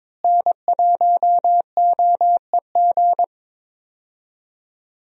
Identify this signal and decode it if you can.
Morse code, heard as a keyed tone, about 21 wpm, 700 Hz. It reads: N1OEG
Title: USA Call Signs - 5 Characters - Repeated 2x - 22wpm